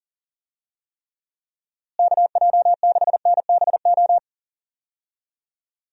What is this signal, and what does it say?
Morse code, heard as a keyed tone, about 40 wpm, 700 Hz. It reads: KJ6NBO
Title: USA Call Signs - 6 Characters - Repeated 2x - 40wpm